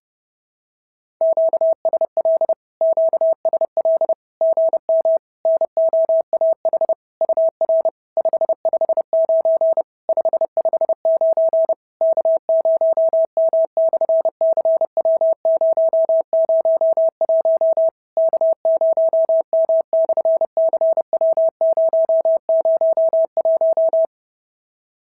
Morse code, heard as a keyed tone, about 30 wpm, 650 Hz. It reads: QSL QSL GM NOAH UR 559 559 K0M/CW001 K0M/CW001
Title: SOTA QSO Practice 30wpm